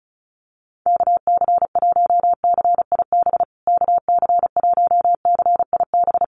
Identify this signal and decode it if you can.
Morse code, heard as a keyed tone, about 34 wpm, 700 Hz. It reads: KC1CIB KC1CIB
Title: USA Call Signs Repeated 3x 35wpm